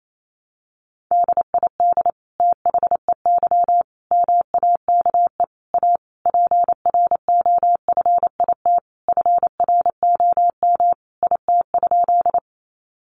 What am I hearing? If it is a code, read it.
Morse code, heard as a keyed tone, about 28 wpm, 700 Hz. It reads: DID THEY MAKE A PROFIT FROM IT?
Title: Sentences from Top 800 Words 28wpm